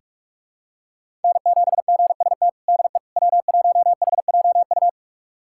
Morse code, heard as a keyed tone, about 41 wpm, 700 Hz. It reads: N7GST DE W1HJU
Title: USA Callsign DE Callsign - 5 Characters - 45wpm